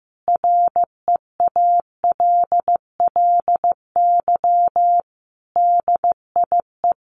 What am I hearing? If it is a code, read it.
Morse code, heard as a keyed tone, about 15 wpm, 700 Hz. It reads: REALLY DIE
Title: Sentences from Top 1300 Words 15wpm